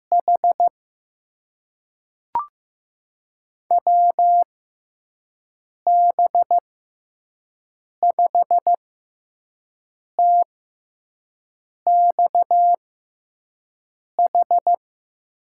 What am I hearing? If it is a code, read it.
Morse code, heard as a keyed tone, about 15 wpm, spaced as for 5 wpm, 700 Hz. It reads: H WB5TXH
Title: USA Call Signs - 6 Characters - 6x Character Spacing - 15wpm